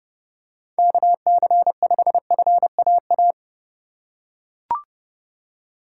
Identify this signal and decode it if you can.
Morse code, heard as a keyed tone, about 30 wpm, 700 Hz. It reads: KC5FAA